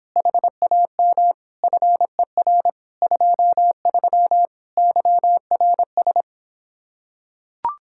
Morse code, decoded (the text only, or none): HAM FER 23 YRS